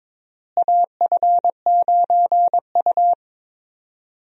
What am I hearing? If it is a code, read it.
Morse code, heard as a keyed tone, about 21 wpm, 700 Hz. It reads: AF9U
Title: USA Call Signs - 4 Characters - Repeated 2x - 22wpm